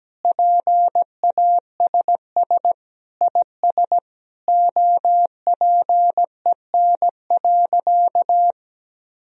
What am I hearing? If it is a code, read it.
Morse code, heard as a keyed tone, about 17 wpm, 700 Hz. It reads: PASS IS OPEN.
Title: Sentences from Top 1000 Words - Review Part 1 - 17wpm